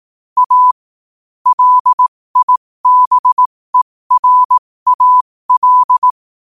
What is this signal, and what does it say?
Morse code, heard as a keyed tone, about 18 wpm, spaced as for 15 wpm, 1000 Hz. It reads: A LIBERAL
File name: qotdfiles / 15wpm